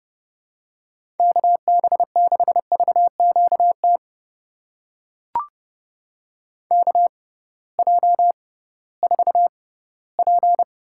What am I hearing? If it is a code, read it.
Morse code, heard as a keyed tone, about 30 wpm, spaced as for 18 wpm, 700 Hz. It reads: KB6VQT KJ4P
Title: USA Call Signs - 6 Characters - 6x Character Spacing - 30wpm